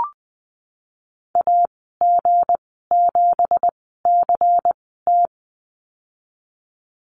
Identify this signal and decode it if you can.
Morse code, heard as a keyed tone, about 20 wpm, spaced as for 15 wpm, 700 Hz. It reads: AG7CT
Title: 167 Lesson 27g - Number 7 - Call Signs 20wpm